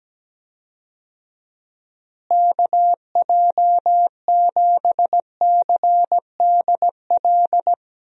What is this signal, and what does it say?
Morse code, heard as a keyed tone, about 17 wpm, 700 Hz. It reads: KJ7CDL
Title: USA Call Signs - 6 Characters - Repeated 3x - 17wpm